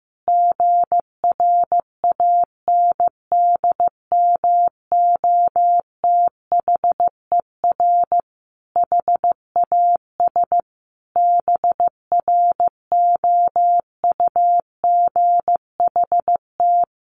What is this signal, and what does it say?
Morse code, heard as a keyed tone, about 14 wpm, 700 Hz. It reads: GRANDMOTHER HAS BROUGHT
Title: Sentences from Top 1200 Words 15wpm